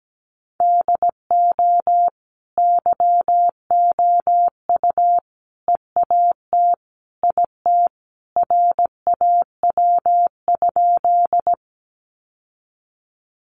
Morse code, heard as a keyed tone, about 17 wpm, 700 Hz. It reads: DO YOU EAT IT RAW?
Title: Sentences from Top 1500 Words 17wpm